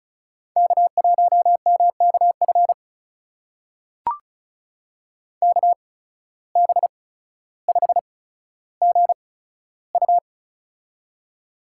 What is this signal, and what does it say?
Morse code, heard as a keyed tone, about 36 wpm, spaced as for 15 wpm, 700 Hz. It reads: K1MKF KB5GU
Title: USA Call Signs - 5 Characters - 8x Character Spacing - 35wpm